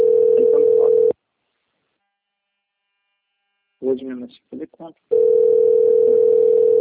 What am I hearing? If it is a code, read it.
Ringback tone (North America), 440 + 480 Hz, with speech.
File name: rg-1600-unknown- - - .41774